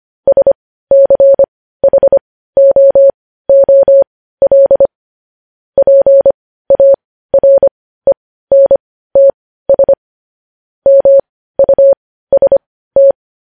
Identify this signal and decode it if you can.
Morse code, heard as a keyed tone, about 25 wpm, spaced as for 15 wpm, 550 Hz. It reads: SCHOOL PARENTS MUST
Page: news headlines in morse code at 15 wpm